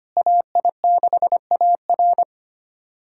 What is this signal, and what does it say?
Morse code, heard as a keyed tone, about 25 wpm, 700 Hz. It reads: AI6AR
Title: USA Callsign DE Callsign - 5 Characters - 25wpm